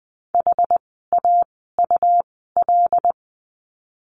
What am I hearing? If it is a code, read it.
Morse code, heard as a keyed tone, about 20 wpm, spaced as for 14 wpm, 700 Hz. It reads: HAUL
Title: 101 Lesson 17f - Letter u - Words 20wpm